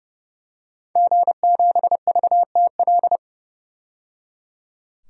Morse code, heard as a keyed tone, about 30 wpm, 700 Hz. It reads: G7VTL
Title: DXCC-100 Call Signs - 5 Characters - 30wpm